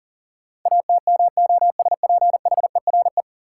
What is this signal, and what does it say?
Morse code, heard as a keyed tone, about 40 wpm, 700 Hz. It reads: ATMOSPHERE
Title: Sentences from Top 1600 Words - New Word Drill Alt - 40wpm